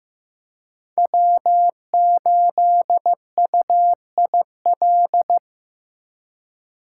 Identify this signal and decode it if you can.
Morse code, heard as a keyed tone, about 15 wpm, 700 Hz. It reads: W8UIL
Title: K1USN SST QSO Practice - 15wpm